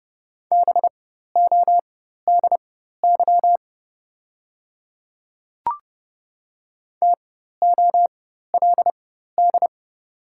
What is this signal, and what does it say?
Morse code, heard as a keyed tone, about 30 wpm, spaced as for 14 wpm, 690 Hz. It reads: BODY TOLD
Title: 4x Character Spacing - Top 500 Words - 30wpm